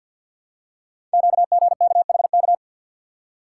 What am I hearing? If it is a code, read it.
Morse code, heard as a keyed tone, about 50 wpm, 700 Hz. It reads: QGKHK
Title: Code Groups - 5 Letters - 50wpm